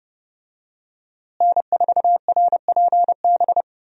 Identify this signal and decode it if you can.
Morse code, heard as a keyed tone, about 32 wpm, 700 Hz. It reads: N4RPB
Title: USA Call Signs - 5 Characters - Repeated 3x - 30wpm